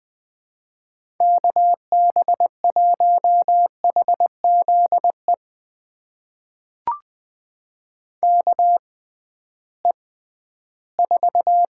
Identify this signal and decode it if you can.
Morse code, heard as a keyed tone, about 20 wpm, spaced as for 13 wpm, 700 Hz. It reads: KB1HZE KE4
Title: USA Call Signs - 6 Characters - 6x Character Spacing - 20wpm